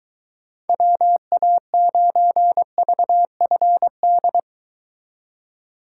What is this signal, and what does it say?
Morse code, heard as a keyed tone, about 23 wpm, 700 Hz. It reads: WA9VFD